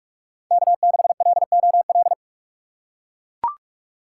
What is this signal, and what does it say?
Morse code, heard as a keyed tone, about 43 wpm, 700 Hz. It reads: K6LOL
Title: USA Call Signs - 5 Characters - 4x Character Spacing - 45wpm